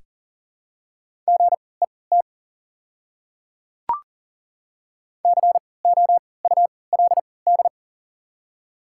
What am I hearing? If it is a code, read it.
Morse code, heard as a keyed tone, about 40 wpm, spaced as for 23 wpm, 700 Hz. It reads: GET COULD
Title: 3x Character Spacing - Top 500 Words - 40wpm